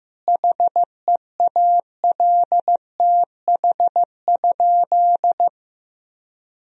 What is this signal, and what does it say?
Morse code, heard as a keyed tone, about 15 wpm, 700 Hz. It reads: HEALTH?
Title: Sentences from Top 1000 Words - Review Part 3 - 15wpm